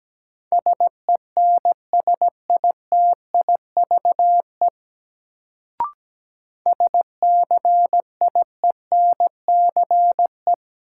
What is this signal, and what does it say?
Morse code, heard as a keyed tone, about 17 wpm, 700 Hz. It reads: SENSITIVE SCIENCE
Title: Sentences from Top 1400 Words - New Word Drill Alt - 17wpm